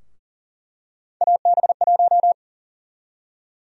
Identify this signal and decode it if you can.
Morse code, heard as a keyed tone, about 40 wpm, 700 Hz. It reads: AB1
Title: USA Call Sign Prefixes - Repeated 2x - 40wpm